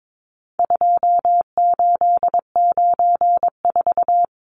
Morse code, heard as a keyed tone, about 22 wpm, 700 Hz. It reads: 2894
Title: Numbers - 4-Digit 22wpm